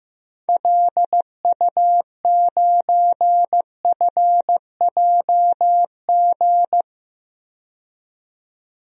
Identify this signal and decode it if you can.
Morse code, heard as a keyed tone, about 15 wpm, 700 Hz. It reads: LU9FJG
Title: DXCC-100 Call Signs - 6 Characters - 15wpm